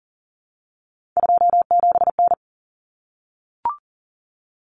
Morse code, heard as a keyed tone, about 40 wpm, 700 Hz. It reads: 27N
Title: Common Cut Numbers - 3-Digit 40wpm